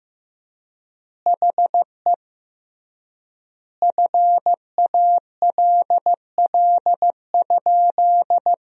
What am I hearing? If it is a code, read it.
Morse code, heard as a keyed tone, about 15 wpm, 700 Hz. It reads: HE FALL?
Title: 3x Word-Spacing - Sentences from Top 500 Words - 15wpm